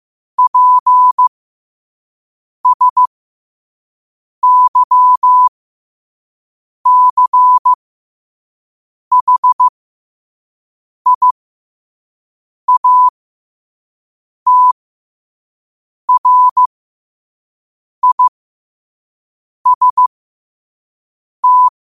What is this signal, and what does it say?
Morse code, heard as a keyed tone, about 15 wpm, spaced as for 5 wpm, 1000 Hz. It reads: PSYCHIATRIST
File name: qotdfiles / 5wpm